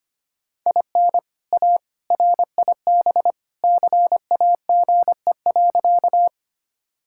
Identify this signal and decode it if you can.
Morse code, heard as a keyed tone, about 25 wpm, 700 Hz. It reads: IN A RIB CAGE.
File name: practice / 25wpm / All-Age Appropriate Jokes - 25wpm